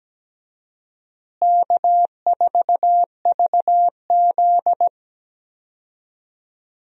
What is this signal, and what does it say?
Morse code, heard as a keyed tone, about 17 wpm, 700 Hz. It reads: K4VZ
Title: USA Call Signs - 4 Characters - Repeated 3x - 17wpm